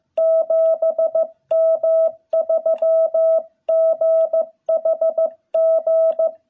Morse code, heard as a keyed tone, about 15 wpm, 660 Hz. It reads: 7M3GHG
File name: jh8cbh / 2021cw